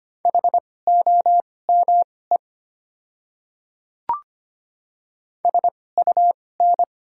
Morse code, heard as a keyed tone, about 25 wpm, spaced as for 18 wpm, 700 Hz. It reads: HOME SUN
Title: 2x Character Spacing - Top 300 Words - 25wpm